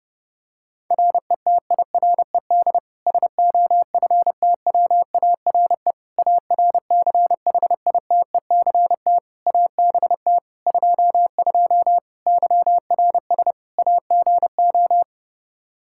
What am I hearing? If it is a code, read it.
Morse code, heard as a keyed tone, about 30 wpm, 700 Hz. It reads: RETIRED SOFTWARE ARCHITECT ABT 22 YRS AGO
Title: Practice QSO - Exchange 3 - More Info - 30wpm